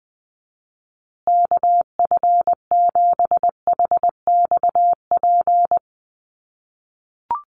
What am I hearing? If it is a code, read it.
Morse code, heard as a keyed tone, about 20 wpm, 700 Hz. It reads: KF7HXP
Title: USA Call Signs Repeated 3x 20wpm